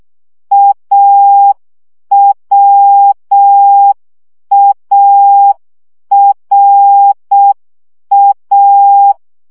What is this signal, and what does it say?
Morse code, heard as a keyed tone, about 6 wpm, 800 Hz. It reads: AWARA
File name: CW gyakorlas a r w 5 baud